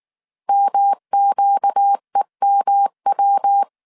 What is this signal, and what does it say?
Morse code, heard as a keyed tone, about 19 wpm, 800 Hz. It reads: MQEMW
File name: CW / DET / 1 6 1500C S C 18wpm 2356